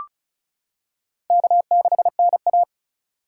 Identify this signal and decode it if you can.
Morse code, heard as a keyed tone, about 36 wpm, 700 Hz. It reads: K6NA
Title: K1USN SST QSO Practice - 35wpm